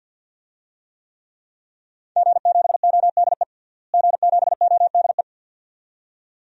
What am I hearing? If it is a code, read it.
Morse code, heard as a keyed tone, about 50 wpm, 700 Hz. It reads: G7ODE G7ODE
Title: DXCC-100 Call Signs - 5 Characters - Repeated 2x - 50wpm